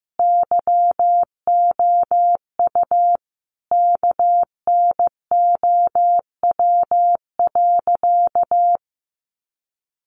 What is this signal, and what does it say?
Morse code, heard as a keyed tone, about 15 wpm, 700 Hz. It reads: YOU KNOW.
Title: Sentences from Top 1200 Words 15wpm